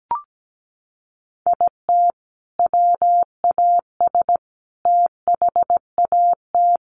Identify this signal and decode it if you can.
Morse code, heard as a keyed tone, about 17 wpm, 700 Hz. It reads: IT WAS THAT
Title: Sentences from Top 1400 Words 17wpm